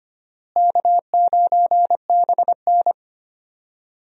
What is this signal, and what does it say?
Morse code, heard as a keyed tone, about 25 wpm, 700 Hz. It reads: K9BN
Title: USA Call Signs - 4 Characters - Repeated 2x - 25wpm